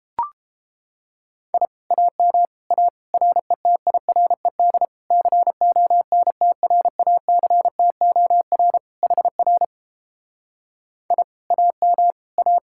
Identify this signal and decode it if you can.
Morse code, heard as a keyed tone, about 32 wpm, 700 Hz. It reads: I AM A RETIRED CONTRACTOR HR I AM A